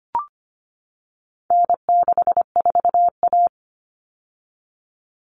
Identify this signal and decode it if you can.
Morse code, heard as a keyed tone, about 25 wpm, 700 Hz. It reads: N64A
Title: Common Cut Numbers - 4-Digit 25wpm